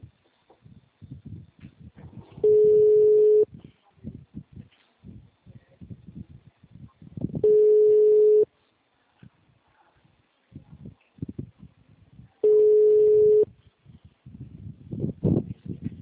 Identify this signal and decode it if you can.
Ringback tone (Europe, 425 Hz).
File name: out- -2013- - - .149